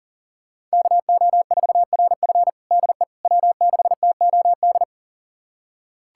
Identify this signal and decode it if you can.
Morse code, heard as a keyed tone, about 40 wpm, 700 Hz. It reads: KO4RF DE W6TOD
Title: USA Callsign DE Callsign - 5 Characters - 40wpm